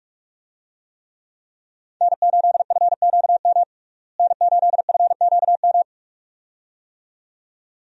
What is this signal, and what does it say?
Morse code, heard as a keyed tone, about 44 wpm, 700 Hz. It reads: N8FQM N8FQM
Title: USA Call Signs - 5 Characters - Repeated 2x - 45wpm